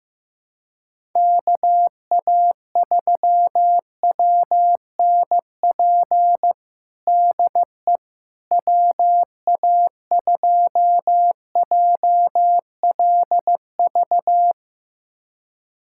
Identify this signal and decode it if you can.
Morse code, heard as a keyed tone, about 15 wpm, 700 Hz. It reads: KA3WNP DE WA2JLV
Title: USA Callsign DE Callsign - 6 Characters - 15wpm